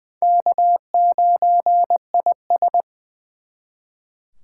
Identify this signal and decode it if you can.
Morse code, heard as a keyed tone, about 24 wpm, 700 Hz. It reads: K9IS
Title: Speed Racing - USA 1x2 Call Signs 20wpm